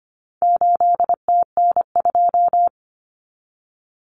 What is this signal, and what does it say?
Morse code, heard as a keyed tone, about 25 wpm, 700 Hz. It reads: 8TN2
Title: Common Cut Numbers - 4-Digit 25wpm